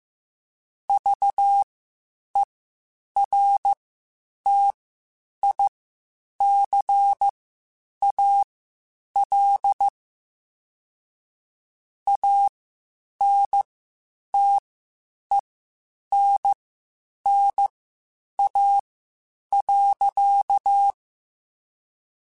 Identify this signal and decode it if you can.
Morse code, heard as a keyed tone, about 15 wpm, spaced as for 8 wpm, 800 Hz. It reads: VERTICAL ANTENNA.
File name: QSOs Spaced-mp3s/kc7haa @13 sp